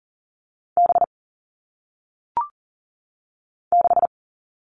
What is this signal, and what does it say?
Morse code, heard as a keyed tone, about 39 wpm, 700 Hz. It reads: B 6
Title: All Cut Numbers - 1-Digit 40wpm